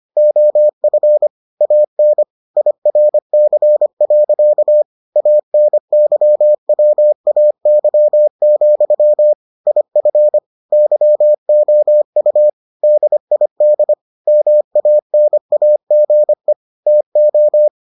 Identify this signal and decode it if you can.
Morse code, heard as a keyed tone, about 25 wpm, 600 Hz. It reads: OF AN IRC. ANYWAY, IF YOU DID MANAGE TO